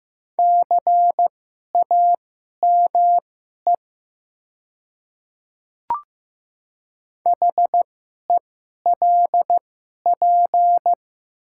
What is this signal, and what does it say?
Morse code, heard as a keyed tone, about 15 wpm, spaced as for 11 wpm, 700 Hz. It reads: CAME HELP
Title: 2x Character Spacing - Top 200 Words - 15wpm